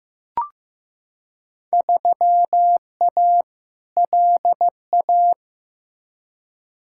Morse code, heard as a keyed tone, about 15 wpm, 700 Hz. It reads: 3A LA
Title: Field Day Exchanges - 15wpm